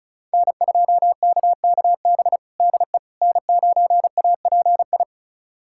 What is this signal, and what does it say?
Morse code, heard as a keyed tone, about 34 wpm, 690 Hz. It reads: N2KKB DE N9API
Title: USA Callsign DE Callsign - 5 Characters - 35wpm